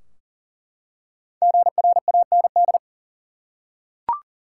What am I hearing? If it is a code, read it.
Morse code, heard as a keyed tone, about 40 wpm, 700 Hz. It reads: GRAND
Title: Five Letter Words - 40wpm